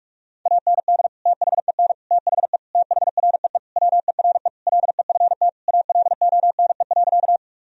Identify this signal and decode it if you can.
Morse code, heard as a keyed tone, about 43 wpm, 700 Hz. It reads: AND THEN THE THREE WERE LEFT ALONE.
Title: Sentences from Top 1000 Words - Review Part 1 - 45wpm